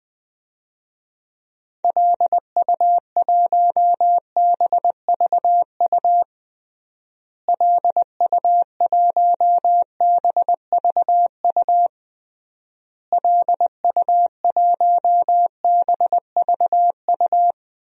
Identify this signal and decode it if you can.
Morse code, heard as a keyed tone, about 20 wpm, 700 Hz. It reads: LU1BVU LU1BVU LU1BVU
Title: DXCC-100 Call Signs - 6 Characters - Repeated 3x - 20wpm